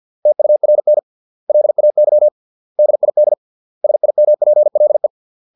Morse code, heard as a keyed tone, about 50 wpm, 600 Hz. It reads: TURN LAY DID SIMPLE